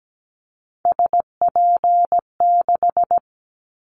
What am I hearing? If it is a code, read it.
Morse code, heard as a keyed tone, about 17 wpm, 700 Hz. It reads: SP6
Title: Callsign Prefixes for Top 25 DXCC - 17wpm